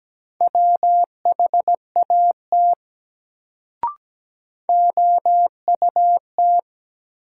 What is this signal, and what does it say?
Morse code, heard as a keyed tone, about 17 wpm, 700 Hz. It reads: WHAT OUT
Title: Sets of 2 Words - New Word Drill - 17wpm